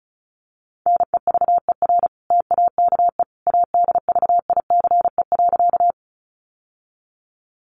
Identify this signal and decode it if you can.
Morse code, heard as a keyed tone, about 34 wpm, 700 Hz. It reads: NEVER TAKE ADVICE.
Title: Sentences from Top 1100 Words 35wpm